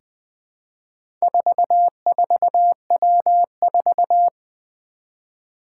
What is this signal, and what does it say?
Morse code, heard as a keyed tone, about 20 wpm, 700 Hz. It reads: 44W4
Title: Four Challenging Letter-Number Combinations 20wpm